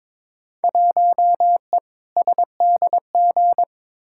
Morse code, heard as a keyed tone, about 22 wpm, 700 Hz. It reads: 1E SDG
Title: Field Day Exchanges - 22wpm